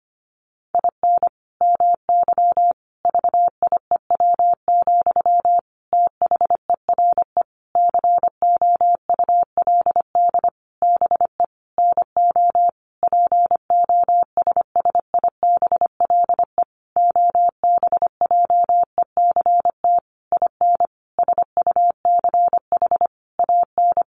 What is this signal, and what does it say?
Morse code, heard as a keyed tone, about 25 wpm, 700 Hz. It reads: IN MY VIEW, THERE COULD BE NO POSSIBLE OBJECT IN SUCH AN
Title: Sentences from Top 1500 Words 25wpm